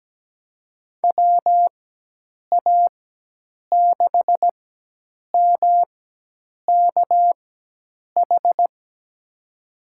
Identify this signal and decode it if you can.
Morse code, heard as a keyed tone, about 17 wpm, spaced as for 8 wpm, 700 Hz. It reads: WA6MKH
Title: USA Call Signs - 6 Characters - 4x Character Spacing - 17wpm